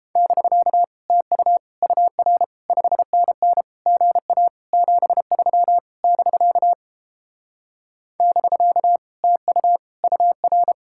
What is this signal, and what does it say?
Morse code, heard as a keyed tone, about 29 wpm, 700 Hz. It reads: <BK> TU UR 5NN GA 73 <BK> <BK> TU UR